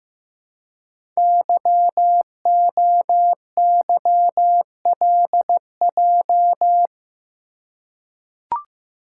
Morse code, heard as a keyed tone, about 15 wpm, 700 Hz. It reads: YOYLJ
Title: Code Groups - 5 Letter:Numbers - 15wpm